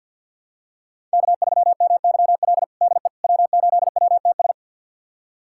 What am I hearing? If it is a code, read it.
Morse code, heard as a keyed tone, about 50 wpm, 700 Hz. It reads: K3MYL DE W8WTS
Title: USA Callsign DE Callsign - 5 Characters - 50wpm